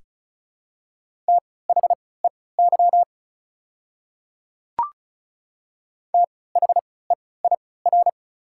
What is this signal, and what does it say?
Morse code, heard as a keyed tone, about 36 wpm, spaced as for 20 wpm, 690 Hz. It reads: THEY THEIR
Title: 3x Character Spacing - Top 100 Words - 35wpm